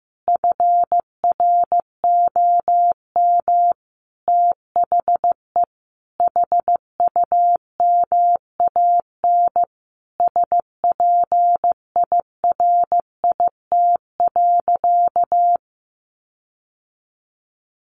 Morse code, heard as a keyed tone, about 15 wpm, 700 Hz. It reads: FROM THE HUMAN SPIRIT.
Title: Sentences from Top 1400 Words 15wpm